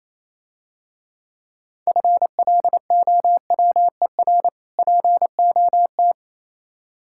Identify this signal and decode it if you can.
Morse code, heard as a keyed tone, about 27 wpm, 690 Hz. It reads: FLOWER POT
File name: Easy Words - Speed-Racing with Hints - 22wpm